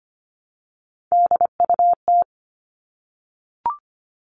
Morse code, heard as a keyed tone, about 25 wpm, 700 Hz. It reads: DUT
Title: All Cut Numbers - 3-Digit 25wpm